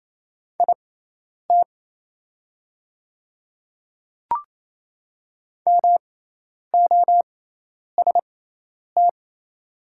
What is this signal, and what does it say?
Morse code, heard as a keyed tone, about 27 wpm, spaced as for 10 wpm, 700 Hz. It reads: IT MOST
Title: 6x Character Spacing - Top 100 Words - 28wpm